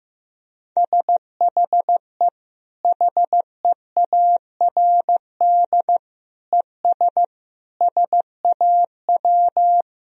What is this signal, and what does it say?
Morse code, heard as a keyed tone, about 15 wpm, 700 Hz. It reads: SHE HEARD ES SAW
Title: Sentences from Top 500 Words and Abbreviations 15wpm